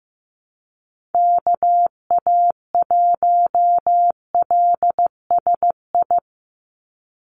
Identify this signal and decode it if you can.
Morse code, heard as a keyed tone, about 15 wpm, 700 Hz. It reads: KA1LSI
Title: USA Call Signs 15wpm